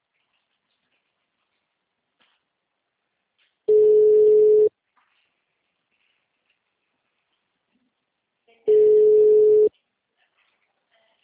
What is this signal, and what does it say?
Ringback tone (Europe, 425 Hz).